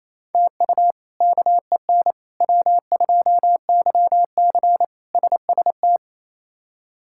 Morse code, heard as a keyed tone, about 28 wpm, 700 Hz. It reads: TU KEN W2YC SST
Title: K1USN SST QSO Practice - 28wpm